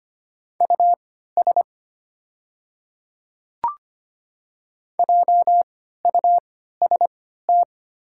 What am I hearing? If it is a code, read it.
Morse code, heard as a keyed tone, about 25 wpm, spaced as for 14 wpm, 690 Hz. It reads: US JUST
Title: 3x Character Spacing - Top 200 Words - 25wpm